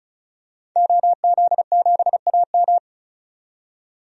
Morse code, heard as a keyed tone, about 36 wpm, 700 Hz. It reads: OZ7AM
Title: DXCC-100 Call Signs - 5 Characters - 35wpm